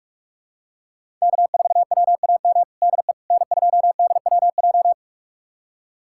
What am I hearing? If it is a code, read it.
Morse code, heard as a keyed tone, about 42 wpm, 700 Hz. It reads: K4WAM DE N2DWJ
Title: USA Callsign DE Callsign - 5 Characters - 45wpm